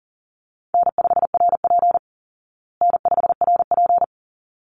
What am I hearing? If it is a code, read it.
Morse code, heard as a keyed tone, about 37 wpm, 700 Hz. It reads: N5RP N5RP
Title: Call Signs Repeated 3x - 40wpm